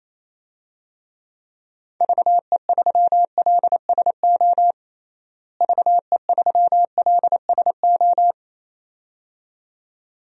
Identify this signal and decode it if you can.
Morse code, heard as a keyed tone, about 27 wpm, 700 Hz. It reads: VE3LSO VE3LSO
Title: DXCC-100 Call Signs - 6 Characters - Repeated 2x - 28wpm